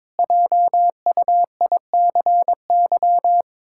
Morse code, heard as a keyed tone, about 22 wpm, 700 Hz. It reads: JUICY
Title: Pangram Challenge v2 - 22wpm